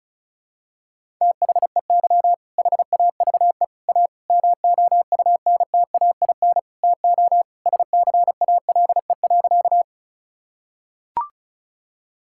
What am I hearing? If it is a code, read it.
Morse code, heard as a keyed tone, about 36 wpm, 700 Hz. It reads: THEY HAVE A MOUNTAIN TO SCALE.
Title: Sentences from Top 1000 Words - Review Part 5 - 35wpm